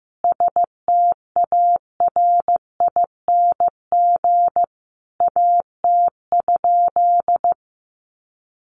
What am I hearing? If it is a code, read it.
Morse code, heard as a keyed tone, about 15 wpm, 700 Hz. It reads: STARING AT?
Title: A Sound Of Thunder 15wpm